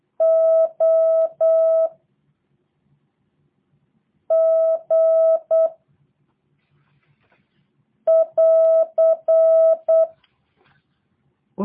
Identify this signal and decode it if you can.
Morse code, heard as a keyed tone, about 8 wpm, 650 Hz. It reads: O G <AR>
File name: jh8cbh / 2023cw / review1-4